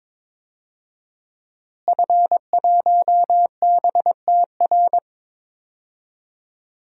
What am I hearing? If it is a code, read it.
Morse code, heard as a keyed tone, about 22 wpm, 700 Hz. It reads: F1BTR
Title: DXCC-100 Call Signs - 5 Characters - Repeated 3x - 22wpm